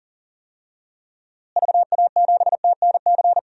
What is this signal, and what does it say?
Morse code, heard as a keyed tone, about 37 wpm, 700 Hz. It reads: VA7TNC
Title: DXCC-100 Call Signs - 6 Characters - Repeated 3x - 40wpm